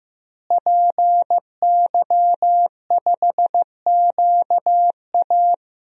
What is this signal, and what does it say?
Morse code, heard as a keyed tone, about 16 wpm, 700 Hz. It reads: PY5QA
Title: DXCC-100 Call Signs - 5 Characters - Repeated 3x - 15wpm